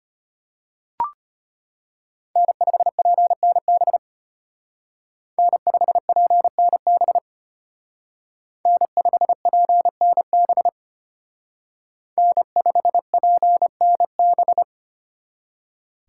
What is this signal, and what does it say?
Morse code, heard as a keyed tone, about 32 wpm, 700 Hz. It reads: N5PNB N5PNB N5PNB N5PNB